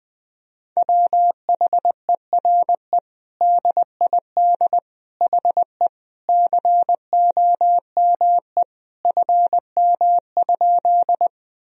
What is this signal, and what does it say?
Morse code, heard as a keyed tone, about 20 wpm, 700 Hz. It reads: WHERE DID HE COME FM?
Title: Sentences from Top 200 Words and Abbreviations 20wpm